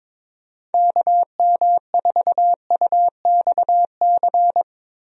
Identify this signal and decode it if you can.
Morse code, heard as a keyed tone, about 22 wpm, 700 Hz. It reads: KM4UXC
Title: USA Call Signs - 6 Characters - Repeated 2x - 22wpm